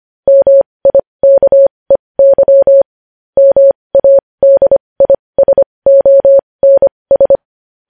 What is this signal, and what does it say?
Morse code, heard as a keyed tone, about 25 wpm, spaced as for 20 wpm, 550 Hz. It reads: MIKEY MADISONS
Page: news-headlines-in-morse-code-at-20-wpm